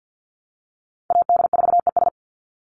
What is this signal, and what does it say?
Morse code, heard as a keyed tone, about 49 wpm, 700 Hz. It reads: AD4ES
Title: Call Signs Repeated 3x - 50wpm